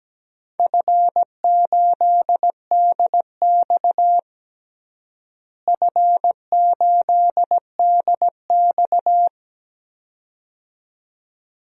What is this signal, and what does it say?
Morse code, heard as a keyed tone, about 17 wpm, 700 Hz. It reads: F8DX F8DX
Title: DXCC-100 Call Signs - 4 Characters - Repeated 2x - 17wpm